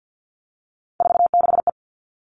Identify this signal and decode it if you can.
Morse code, heard as a keyed tone, about 50 wpm, 700 Hz. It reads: 46E
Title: All Cut Numbers - 3-Digit 50wpm